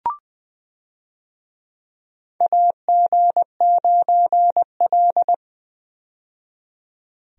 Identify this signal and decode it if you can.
Morse code, heard as a keyed tone, about 20 wpm, 700 Hz. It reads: AG9L
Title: USA Call Signs - 4 Characters - 20wpm